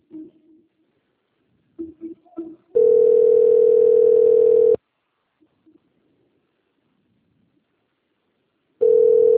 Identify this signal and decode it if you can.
Ringback tone (North America), 440 + 480 Hz.